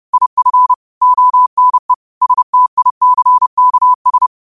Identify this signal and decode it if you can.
Morse code, heard as a keyed tone, about 30 wpm, 1000 Hz. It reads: IF ONE STICKS
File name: qotdfiles / 30wpm